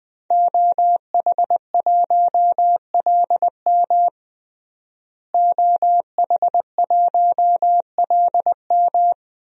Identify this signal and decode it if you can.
Morse code, heard as a keyed tone, about 20 wpm, 700 Hz. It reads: OH1LM OH1LM
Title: DXCC-100 Call Signs - 5 Characters - Repeated 2x - 20wpm